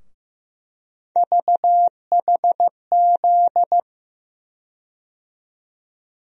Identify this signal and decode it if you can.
Morse code, heard as a keyed tone, about 15 wpm, 700 Hz. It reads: VHZ
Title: USA Call Sign Postfixes - 3 Letters 15wpm